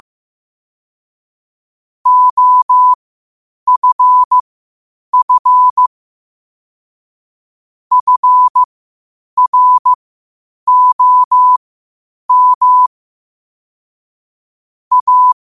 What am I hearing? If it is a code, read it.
Morse code, heard as a keyed tone, about 15 wpm, spaced as for 8 wpm, 1000 Hz. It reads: OFF FROM A
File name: qotdfiles / 8wpm